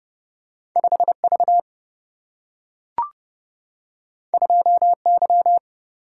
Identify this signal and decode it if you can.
Morse code, heard as a keyed tone, about 31 wpm, 700 Hz. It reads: 5V 2Y
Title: Single-Letter Single-Number 30wpm